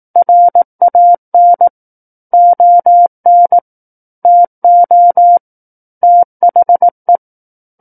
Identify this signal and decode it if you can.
Morse code, heard as a keyed tone, about 18 wpm, 700 Hz. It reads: RAN ON TO THE
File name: 18 WPM